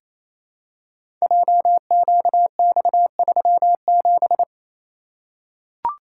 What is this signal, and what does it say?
Morse code, heard as a keyed tone, about 27 wpm, 700 Hz. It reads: JQX37
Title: Code Groups - 5 Letter:Numbers - 28wpm